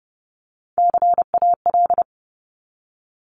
Morse code, heard as a keyed tone, about 30 wpm, 700 Hz. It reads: CAL
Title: Top Three Letter Combinations in Words 30wpm